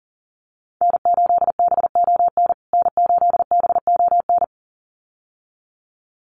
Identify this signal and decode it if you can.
Morse code, heard as a keyed tone, about 40 wpm, 700 Hz. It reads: N8BON N8BON
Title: USA Call Signs Repeated 2x 40wpm